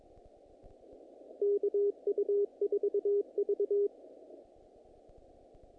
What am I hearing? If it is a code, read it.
Morse code, heard as a keyed tone, about 22 wpm, 410 Hz. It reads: KU4V
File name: contests / naqp-cw-k5tr-op-k5pi- -2000